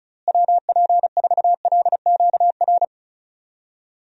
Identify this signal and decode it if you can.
Morse code, heard as a keyed tone, about 36 wpm, 700 Hz. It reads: WP4LQR
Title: USA Call Signs - 6 Characters - Repeated 3x - 35wpm